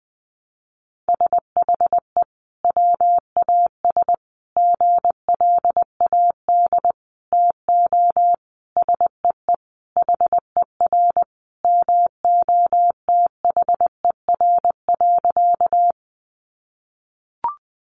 Morse code, heard as a keyed tone, about 20 wpm, 700 Hz. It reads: SHE WAS GLAD TO SEE HER MOTHER.
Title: Sentences from Top 800 Words 20wpm